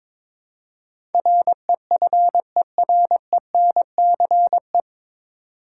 Morse code, heard as a keyed tone, about 22 wpm, 700 Hz. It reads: REFERENCE
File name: 9 Letter Words - First 3 Letters Given - 22wpm